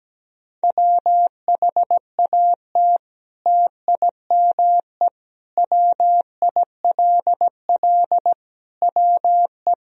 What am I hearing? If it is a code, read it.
Morse code, heard as a keyed tone, about 17 wpm, 700 Hz. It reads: WHAT TIME WILL WE
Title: Sentences from Top 1000 Words - Review Part 1 - 17wpm